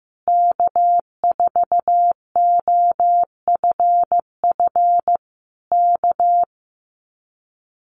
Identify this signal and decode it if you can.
Morse code, heard as a keyed tone, about 15 wpm, 700 Hz. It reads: K4OFF K
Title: Common QSO Phrases Encore 15wpm